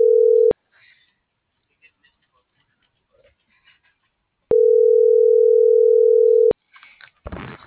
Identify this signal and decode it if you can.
Ringback tone (North America), 440 + 480 Hz.